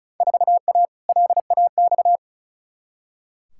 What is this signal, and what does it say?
Morse code, heard as a keyed tone, about 34 wpm, 700 Hz. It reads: 4A LAX
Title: Field Day Exchanges - 35wpm